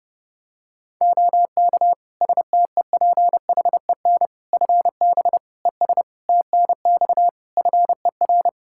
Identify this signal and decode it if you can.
Morse code, heard as a keyed tone, about 32 wpm, 700 Hz. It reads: OK STEPHEN FB ES TNX FER
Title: K1USN SST QSO Practice 30wpm